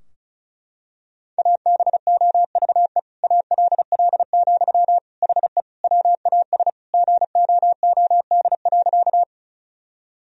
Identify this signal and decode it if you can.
Morse code, heard as a keyed tone, about 36 wpm, 700 Hz. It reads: ABOVE ALL, HE WAS GOOD.
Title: Sentences from Top 1000 Words - Review Part 2 - 35wpm